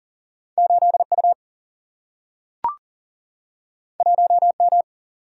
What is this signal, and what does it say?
Morse code, heard as a keyed tone, about 39 wpm, 700 Hz. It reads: 8U 1M
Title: Single-Letter Single-Number 40wpm